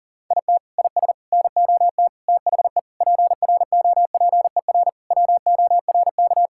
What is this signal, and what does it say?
Morse code, heard as a keyed tone, about 40 wpm, 700 Hz. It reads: IT IS NOT THE PROPER WORK
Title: Sentences from Top 1000 Words - Review Part 5 - 40wpm